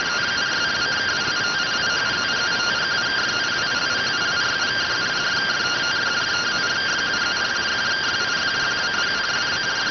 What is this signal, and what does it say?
Two-tone teleprinter signal, about 170 Hz shift.